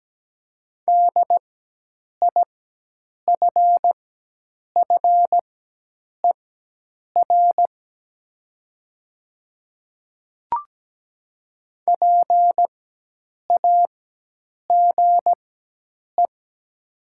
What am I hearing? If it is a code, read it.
Morse code, heard as a keyed tone, about 17 wpm, spaced as for 8 wpm, 700 Hz. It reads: DIFFER PAGE
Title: 4x Character Spacing - Top 300 Words - 17wpm